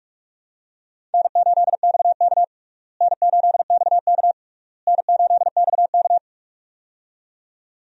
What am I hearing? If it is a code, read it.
Morse code, heard as a keyed tone, about 46 wpm, 700 Hz. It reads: N8XK N8XK N8XK
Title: USA Call Signs - 4 Characters - Repeated 3x - 45wpm